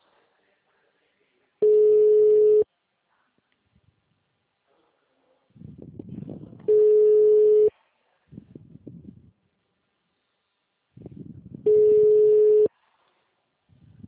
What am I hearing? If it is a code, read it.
Ringback tone (Europe, 425 Hz).